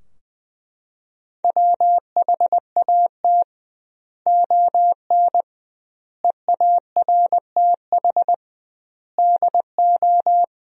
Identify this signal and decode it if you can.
Morse code, heard as a keyed tone, about 20 wpm, 700 Hz. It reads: WHAT ON EARTH DO
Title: 2x Word-Spacing - Sentences from Top 200 Words - 20wpm